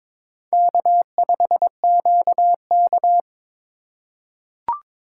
Morse code, heard as a keyed tone, about 22 wpm, 690 Hz. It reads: K5QK
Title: USA Call Signs - 4 Characters - 3x Character Spacing - 22wpm